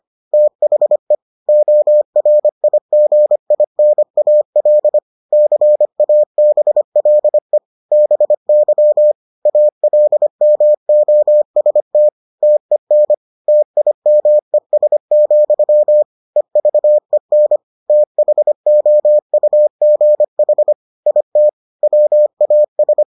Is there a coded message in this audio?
Yes: Morse code from 0.3 to 23.0 s, heard as a keyed tone, about 25 wpm, 600 Hz. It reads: THE ORIGINAL CABLE BY ALMOST TEN TIMES, EVEN THOUGH IT WAS